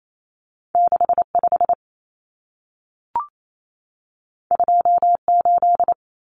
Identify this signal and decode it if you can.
Morse code, heard as a keyed tone, about 28 wpm, 700 Hz. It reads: 65 28
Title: Numbers - 2-Digit 28wpm